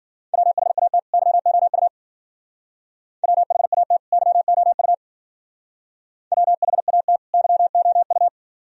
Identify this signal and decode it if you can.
Morse code, heard as a keyed tone, about 50 wpm, 700 Hz. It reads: WHAT YOU WHAT YOU WHAT YOU
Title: Speed-Racing - Sets of 2 Words 40wpm